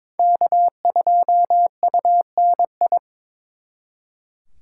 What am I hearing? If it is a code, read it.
Morse code, heard as a keyed tone, about 26 wpm, 700 Hz. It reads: K2UNI
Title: Speed Racing - USA 1x3 Call Signs 22wpm